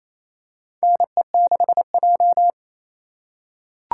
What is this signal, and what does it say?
Morse code, heard as a keyed tone, about 28 wpm, 700 Hz. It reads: NE6J
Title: USA Call Signs - 3x Character Spacing - 28wpm